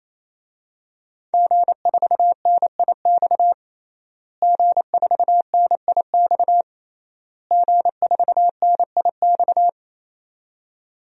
Morse code, heard as a keyed tone, about 28 wpm, 700 Hz. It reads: G4NIX G4NIX G4NIX
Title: DXCC-100 Call Signs - 5 Characters - Repeated 3x - 28wpm